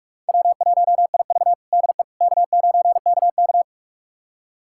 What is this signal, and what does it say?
Morse code, heard as a keyed tone, about 41 wpm, 700 Hz. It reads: W1IV DE K9KK
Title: USA Callsign DE Callsign - 4 Characters - 45wpm